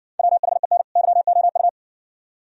Morse code, heard as a keyed tone, about 55 wpm, 700 Hz. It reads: WHEN YOU